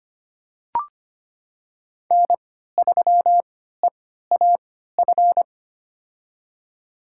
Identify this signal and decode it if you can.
Morse code, heard as a keyed tone, about 25 wpm, spaced as for 14 wpm, 700 Hz. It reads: N3EAF
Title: USA Call Signs - 5 Characters - 3x Character Spacing - 25wpm